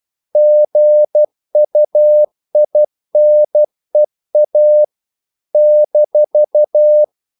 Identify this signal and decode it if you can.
Morse code, heard as a keyed tone, about 12 wpm, 600 Hz. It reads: GUINEA -
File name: CW / Practice / NationalCapitals12